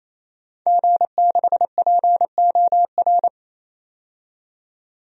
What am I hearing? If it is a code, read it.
Morse code, heard as a keyed tone, about 28 wpm, 700 Hz. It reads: G6POR
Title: DXCC-100 Call Signs - 5 Characters - 28wpm